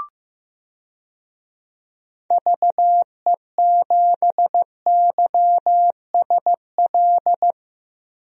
Morse code, heard as a keyed tone, about 16 wpm, 700 Hz. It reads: VE7YSL
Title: DXCC-100 Call Signs - 6 Characters - Repeated 2x - 15wpm